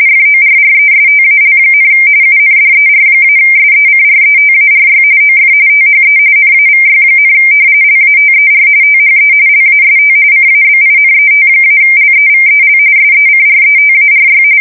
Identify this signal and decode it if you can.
RTTY, about 45.45 baud. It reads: FORRESTAL, UNDER-SECRETARY OF THE NAVY, LATER THE LAST CABINET-LEVEL SECRETARY
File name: fsk170 / battleshipnj